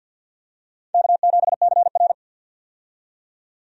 Morse code, heard as a keyed tone, about 52 wpm, 700 Hz. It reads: K7CR
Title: USA Call Signs - 4 Characters - 50wpm